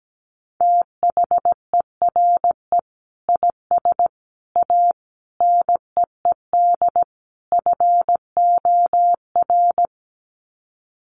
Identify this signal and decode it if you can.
Morse code, heard as a keyed tone, about 17 wpm, 700 Hz. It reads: THERE IS A NEED FOR
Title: Sets of 5 Words Encore2 17wpm